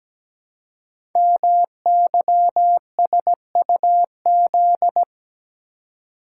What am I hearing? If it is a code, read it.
Morse code, heard as a keyed tone, about 17 wpm, 700 Hz. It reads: MYSUZ
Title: Code Groups - 5 Letters - 17wpm